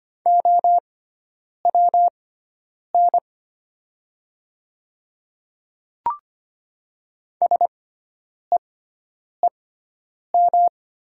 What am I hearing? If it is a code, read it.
Morse code, heard as a keyed tone, about 25 wpm, spaced as for 9 wpm, 700 Hz. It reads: OWN SEEM
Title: 6x Character Spacing - Top 300 Words - 25wpm